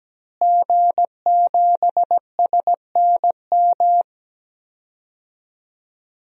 Morse code, heard as a keyed tone, about 17 wpm, 700 Hz. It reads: G7SNM
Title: DXCC-100 Call Signs - 5 Characters - Repeated 2x - 17wpm